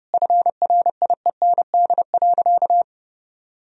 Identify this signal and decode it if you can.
Morse code, heard as a keyed tone, about 30 wpm, 700 Hz. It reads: FRIEND.
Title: Sentences from Top 1000 Words - Review Part 4 - 30wpm